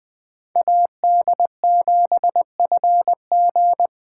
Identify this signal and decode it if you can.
Morse code, heard as a keyed tone, about 20 wpm, 700 Hz. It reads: AD7FG
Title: USA Call Signs - 3x Character Spacing - 20wpm